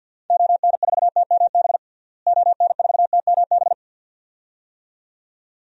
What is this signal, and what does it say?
Morse code, heard as a keyed tone, about 52 wpm, 700 Hz. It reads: ON4TMB ON4TMB
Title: DXCC-100 Call Signs - 6 Characters - Repeated 2x - 50wpm